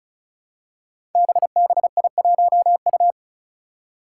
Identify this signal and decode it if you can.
Morse code, heard as a keyed tone, about 36 wpm, 700 Hz. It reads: BBI1U
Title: Five Challenging Letter-Number Combinations 35wpm